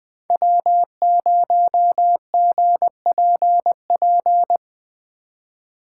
Morse code, heard as a keyed tone, about 20 wpm, 700 Hz. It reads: W0GPP
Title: USA Call Signs - 5 Characters - 3x Character Spacing - 20wpm